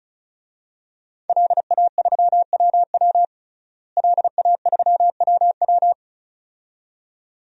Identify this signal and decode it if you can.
Morse code, heard as a keyed tone, about 34 wpm, 700 Hz. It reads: LA3WW LA3WW
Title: DXCC-100 Call Signs - 5 Characters - Repeated 2x - 35wpm